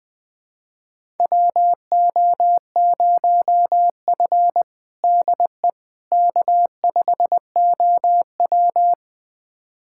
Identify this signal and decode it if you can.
Morse code, heard as a keyed tone, about 20 wpm, 700 Hz. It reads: WO0F DE K5OW
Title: USA Callsign DE Callsign - 4 Characters - 20wpm